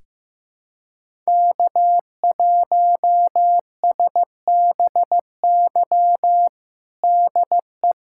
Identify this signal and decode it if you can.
Morse code, heard as a keyed tone, about 15 wpm, 700 Hz. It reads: K1SBY DE
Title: USA Callsign DE Callsign 15wpm